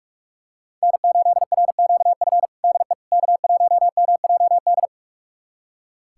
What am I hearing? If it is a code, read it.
Morse code, heard as a keyed tone, about 46 wpm, 700 Hz. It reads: N8RQF DE K1MJD
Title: USA Callsign DE Callsign - 5 Characters - 45wpm